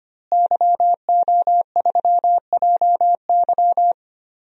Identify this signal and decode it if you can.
Morse code, heard as a keyed tone, about 25 wpm, 700 Hz. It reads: YO3JY
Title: DXCC-100 Call Signs - 5 Characters - Repeated 3x - 25wpm